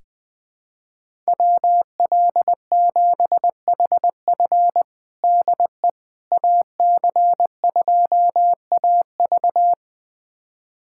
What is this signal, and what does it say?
Morse code, heard as a keyed tone, about 20 wpm, 700 Hz. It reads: WL7HF DE AC2AV
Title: USA Callsign DE Callsign - 5 Characters - 20wpm